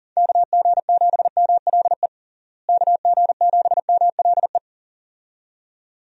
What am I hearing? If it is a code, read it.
Morse code, heard as a keyed tone, about 37 wpm, 700 Hz. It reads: KG7MLE KG7MLE
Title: USA Call Signs - 6 Characters - Repeated 2x - 40wpm